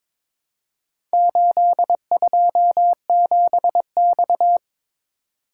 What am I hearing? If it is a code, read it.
Morse code, heard as a keyed tone, about 21 wpm, 700 Hz. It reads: 827X
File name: practice / 22wpm / Four Letter-Number Combinations - 22wpm